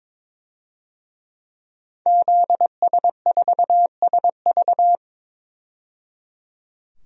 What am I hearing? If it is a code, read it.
Morse code, heard as a keyed tone, about 22 wpm, 700 Hz. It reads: ZS4SV
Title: DXCC-100 Call Signs - 5 Characters - 22wpm